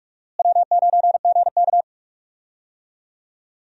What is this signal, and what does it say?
Morse code, heard as a keyed tone, about 46 wpm, 700 Hz. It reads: W9GK
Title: USA Call Signs - 4 Characters - Repeated 3x - 45wpm